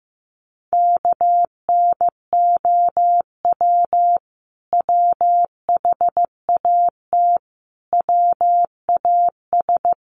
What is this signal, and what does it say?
Morse code, heard as a keyed tone, about 15 wpm, 700 Hz. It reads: KNOW WHAT WAS
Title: Sets of 5 Words Encore2 15wpm